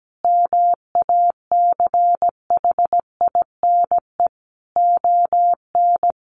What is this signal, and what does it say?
Morse code, heard as a keyed tone, about 17 wpm, 700 Hz. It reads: MACHINE ON
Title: A Sound Of Thunder 17wpm